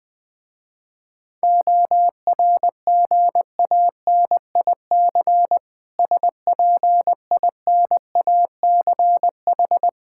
Morse code, heard as a keyed tone, about 20 wpm, 690 Hz. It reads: ORGANIC SPINACH
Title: Difficult Words with Hints - 20wpm